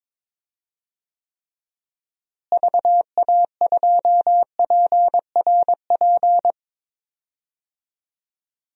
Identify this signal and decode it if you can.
Morse code, heard as a keyed tone, about 22 wpm, 700 Hz. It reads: VA2PRP
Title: DXCC-100 Call Signs - 6 Characters - Repeated 2x - 22wpm